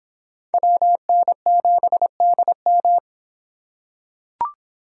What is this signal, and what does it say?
Morse code, heard as a keyed tone, about 26 wpm, 700 Hz. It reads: WN7DM